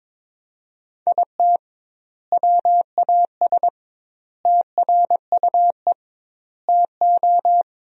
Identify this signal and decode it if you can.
Morse code, heard as a keyed tone, about 22 wpm, 700 Hz. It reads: IT WAS TRUE TO
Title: 2x Word-Spacing - Sentences from Top 400 Words - 22wpm